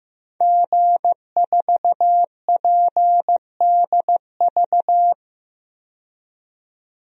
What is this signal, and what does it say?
Morse code, heard as a keyed tone, about 15 wpm, 700 Hz. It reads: G4PDV
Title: DXCC-100 Call Signs - 5 Characters - 15wpm